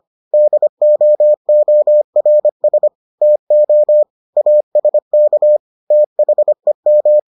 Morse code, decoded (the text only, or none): DOORS TO ASK THEM